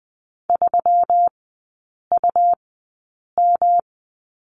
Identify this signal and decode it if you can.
Morse code, heard as a keyed tone, about 20 wpm, spaced as for 8 wpm, 690 Hz. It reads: 3UM
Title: 166 Lesson 27e - Number 7 - Three Characters 20wpm